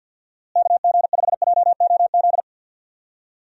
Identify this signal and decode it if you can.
Morse code, heard as a keyed tone, about 52 wpm, 700 Hz. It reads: KG5JOZ
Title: USA Call Signs - 6 Characters - Repeated 2x - 50wpm